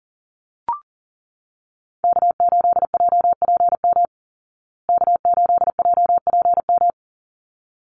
Morse code, heard as a keyed tone, about 40 wpm, 700 Hz. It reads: K8JPM K8JPM
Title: Call Signs Repeated 3x - 40wpm